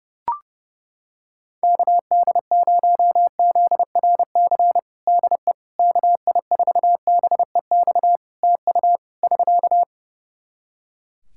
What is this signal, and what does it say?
Morse code, heard as a keyed tone, about 29 wpm, 700 Hz. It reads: KD0ZRC DE KI4BEX TU <SK>
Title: Practice QSO - Exchange 4 - Ending - 30wpm